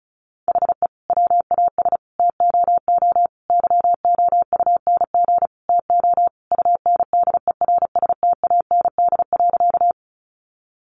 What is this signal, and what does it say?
Morse code, heard as a keyed tone, about 34 wpm, 700 Hz. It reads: HE WAS TOO YOUNG TO UNDERSTAND.
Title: Sentences from Top 600 Words 35wpm